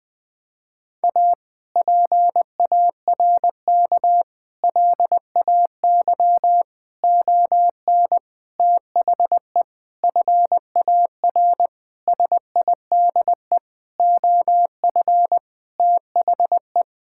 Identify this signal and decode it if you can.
Morse code, heard as a keyed tone, about 20 wpm, 700 Hz. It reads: A PARK LAY ON THE FAR SIDE OF THE
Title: Sentences with 5-Word Phrases - 20wpm